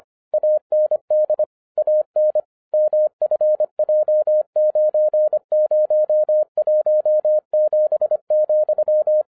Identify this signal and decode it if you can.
Morse code, heard as a keyed tone, about 25 wpm, 610 Hz. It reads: AND AN MFJ9017,
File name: 2011-07-mp3 / n24 QRP OTAKU 25wpm